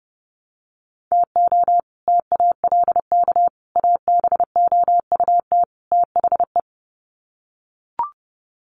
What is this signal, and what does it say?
Morse code, heard as a keyed tone, about 30 wpm, 700 Hz. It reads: TO TALK ABOUT THE
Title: Sets of 4 Words 30wpm